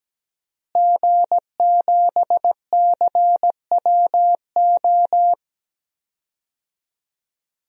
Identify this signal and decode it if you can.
Morse code, heard as a keyed tone, about 17 wpm, 700 Hz. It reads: G7CWO
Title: DXCC-100 Call Signs - 5 Characters - Repeated 2x - 17wpm